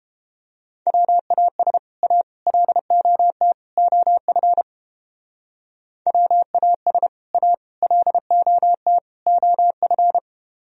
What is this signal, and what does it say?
Morse code, heard as a keyed tone, about 31 wpm, 700 Hz. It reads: WAS A LOT OF WAS A LOT OF